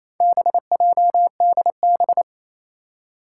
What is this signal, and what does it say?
Morse code, heard as a keyed tone, about 28 wpm, 700 Hz. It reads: BJDB
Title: Four Challenging Letter-Number Combinations 28wpm